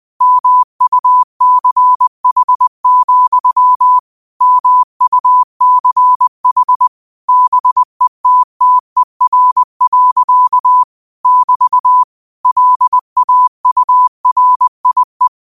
Morse code, heard as a keyed tone, about 20 wpm, 1000 Hz. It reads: MUCH, MUCH BETTER. <BT> LAURIE
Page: Quotes for Thu, 14 Aug 2025 in Morse Code at 20 words per minute.